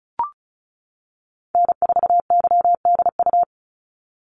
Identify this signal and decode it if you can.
Morse code, heard as a keyed tone, about 36 wpm, 700 Hz. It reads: N4YDU
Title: Call Signs - 35wpm